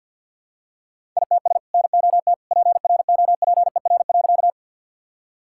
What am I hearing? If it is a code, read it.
Morse code, heard as a keyed tone, about 51 wpm, 700 Hz. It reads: ITS NOT PROPER.
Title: Sentences from Top 1000 Words - Review Part 5 - 50wpm